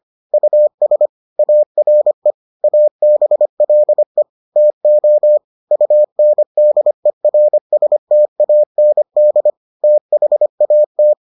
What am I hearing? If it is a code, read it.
Morse code, heard as a keyed tone, about 25 wpm, 600 Hz. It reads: US ARE ABLE TO UNDERSTAND THAT